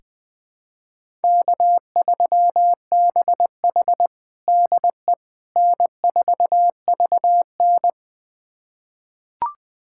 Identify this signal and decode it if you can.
Morse code, heard as a keyed tone, about 20 wpm, 700 Hz. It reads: K3BH DE N4VN
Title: USA Callsign DE Callsign - 4 Characters - 20wpm